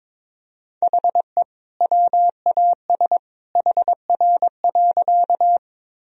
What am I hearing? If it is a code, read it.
Morse code, heard as a keyed tone, about 22 wpm, 700 Hz. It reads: HE WAS HR.
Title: Sentences from Top 200 Words and Abbreviations 22wpm